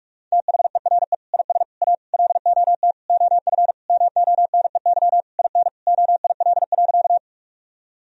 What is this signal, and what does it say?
Morse code, heard as a keyed tone, about 46 wpm, 700 Hz. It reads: THERE IS A LOT OF MONEY IN OIL.
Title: Sentences from Top 1000 Words - Review Part 4 - 45wpm